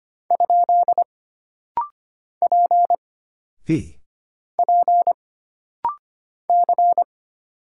Morse code, heard as a keyed tone, about 25 wpm, 700 Hz. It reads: ? P P C
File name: Single Character (Letter:Number:Punctuation:Prosign) - Rapid-Fire - 25wpm